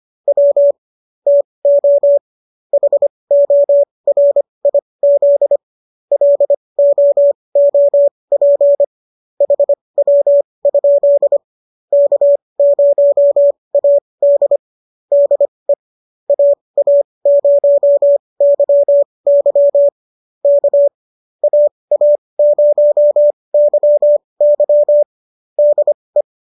Morse code, heard as a keyed tone, about 25 wpm, spaced as for 20 wpm, 570 Hz. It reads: W TO HORIZ LOOP HW? K0AD DE AA0YY K AA0YY DE